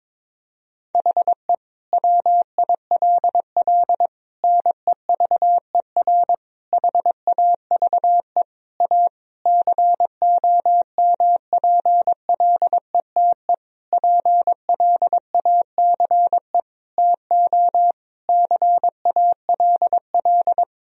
Morse code, heard as a keyed tone, about 22 wpm, 700 Hz. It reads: HE WILL NEVER HAVE A COMPLETE PLACE TO CALL
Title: Sentences from Top 1000 Words - Review Part 2 - 22wpm